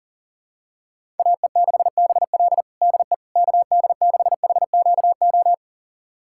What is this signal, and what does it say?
Morse code, heard as a keyed tone, about 40 wpm, 700 Hz. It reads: AE6BL DE KD6HQO
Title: USA Callsign DE Callsign 40wpm